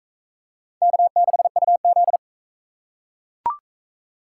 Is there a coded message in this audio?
Yes: Morse code from 0.8 to 2.2 s, heard as a keyed tone, about 42 wpm, 700 Hz. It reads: K6UZ